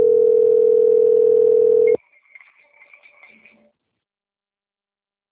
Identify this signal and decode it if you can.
Ringback tone (North America), 440 + 480 Hz.